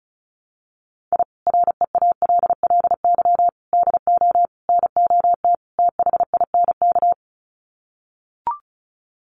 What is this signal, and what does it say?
Morse code, heard as a keyed tone, about 34 wpm, 700 Hz. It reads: I REALLY DO NOT THINK
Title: Sets of 5 Words Encore2 35wpm